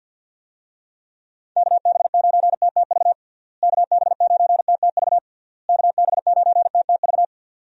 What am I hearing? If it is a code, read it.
Morse code, heard as a keyed tone, about 52 wpm, 700 Hz. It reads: KB9TTV KB9TTV KB9TTV
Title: USA Call Signs - 6 Characters - Repeated 3x - 50wpm